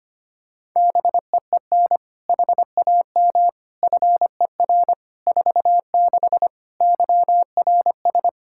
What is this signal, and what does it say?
Morse code, heard as a keyed tone, about 25 wpm, 700 Hz. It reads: BEEN HAM FER 46 YRS
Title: Practice QSO - Exchange 3 - More Info - 25wpm